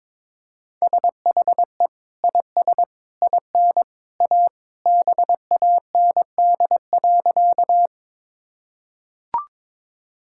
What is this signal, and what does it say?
Morse code, heard as a keyed tone, about 22 wpm, 700 Hz. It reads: SHE IS IN A BAND.
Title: Sentences from Top 1600 Words 22wpm